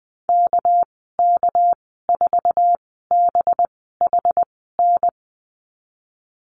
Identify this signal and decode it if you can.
Morse code, heard as a keyed tone, about 20 wpm, spaced as for 14 wpm, 700 Hz. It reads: KK4BHN
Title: 200 Lesson 32g - Letter k - Call Signs 20wpm